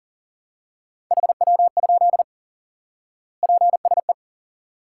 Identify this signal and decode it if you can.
Morse code, heard as a keyed tone, about 37 wpm, 700 Hz. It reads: HW? PSE
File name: IQE Course 3 - Lesson 89 - YAGI - Two Elements - 40wpm